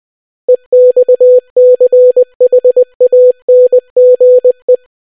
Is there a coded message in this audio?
Yes: Morse code from 0.5 to 4.8 s, heard as a keyed tone, about 20 wpm, 500 Hz. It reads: EXCHANGE